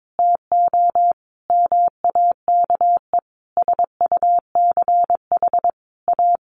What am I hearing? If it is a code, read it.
Morse code, heard as a keyed tone, about 22 wpm, 700 Hz. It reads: TO MAKE SUCH A
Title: Sentences from Top 1200 Words 22wpm